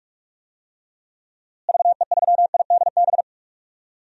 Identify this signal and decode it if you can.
Morse code, heard as a keyed tone, about 42 wpm, 700 Hz. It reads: VE3IDB
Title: DXCC-100 Call Signs - 6 Characters - Repeated 2x - 45wpm